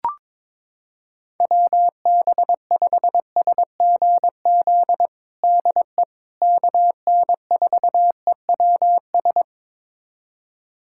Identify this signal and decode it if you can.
Morse code, heard as a keyed tone, about 22 wpm, 700 Hz. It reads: WB5SGZ DE KN4EWS
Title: USA Callsign DE Callsign 22wpm